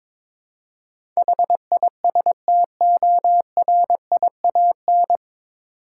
Morse code, heard as a keyed tone, about 23 wpm, 700 Hz. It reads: HISTORIAN
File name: 9 Letter Words - First 3 Letters Given - 22wpm